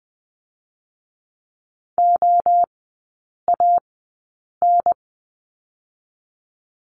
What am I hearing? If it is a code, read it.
Morse code, heard as a keyed tone, about 20 wpm, 700 Hz. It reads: O A N
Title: 023 Lesson 5e - Letter o - Three Characters 20wpm